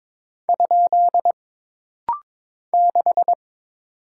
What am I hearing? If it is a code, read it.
Morse code, heard as a keyed tone, about 22 wpm, 700 Hz. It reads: ? 6
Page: This rapid-fire timing is geared to push your ICR (Instant Character Recognition) skills.